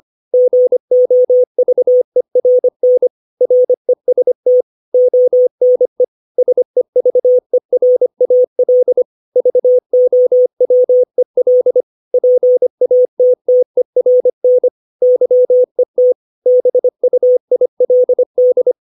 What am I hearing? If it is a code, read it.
Morse code, heard as a keyed tone, about 25 wpm, 500 Hz. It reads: GOVERN REST ONE SEVERAL VOWEL PATTERN YET BUILD
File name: 632 words / 632 Words 25WPM